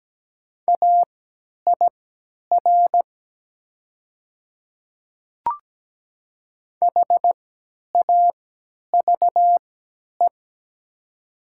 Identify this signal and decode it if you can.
Morse code, heard as a keyed tone, about 17 wpm, spaced as for 10 wpm, 700 Hz. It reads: AIR HAVE
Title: 3x Character Spacing - Top 500 Words - 17wpm